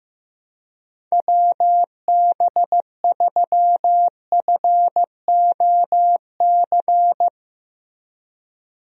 Morse code, heard as a keyed tone, about 15 wpm, 700 Hz. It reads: WB3FOC
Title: USA Call Signs - 6 Characters - Repeated 2x - 15wpm